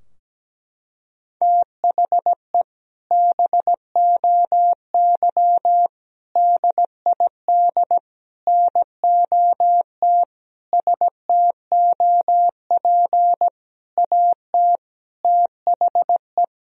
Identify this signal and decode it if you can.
Morse code, heard as a keyed tone, about 17 wpm, 700 Hz. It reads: THE BOY DID NOT STOP AT THE
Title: Sentences from Top 1000 Words - Review Part 2 - 17wpm